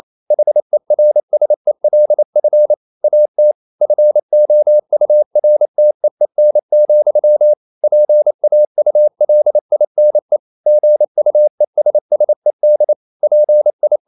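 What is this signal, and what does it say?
Morse code, heard as a keyed tone, about 28 wpm, 600 Hz. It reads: HERSELF AT FOURTEEN, PAULINE GUESSED PI
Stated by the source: Morse code practice at 28 wpm, text of Ken Follett's novel Never in English, from CW med Gnister.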